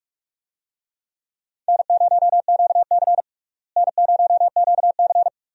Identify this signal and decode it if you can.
Morse code, heard as a keyed tone, about 43 wpm, 700 Hz. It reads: N0QC N0QC
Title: USA Call Signs - 4 Characters - Repeated 3x - 45wpm